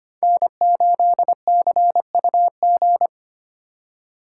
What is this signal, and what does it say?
Morse code, heard as a keyed tone, about 25 wpm, 700 Hz. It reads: N8CUG
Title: USA Call Signs - 5 Characters - 4x Character Spacing - 25wpm